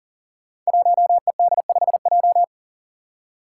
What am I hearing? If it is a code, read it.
Morse code, heard as a keyed tone, about 40 wpm, 700 Hz. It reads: 1ED5J
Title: Five Challenging Letter-Number Combinations 40wpm